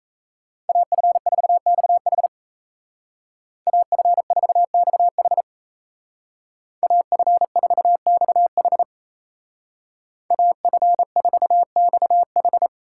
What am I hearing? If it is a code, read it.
Morse code, heard as a keyed tone, about 36 wpm, 700 Hz. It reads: AF4XH AF4XH AF4XH AF4XH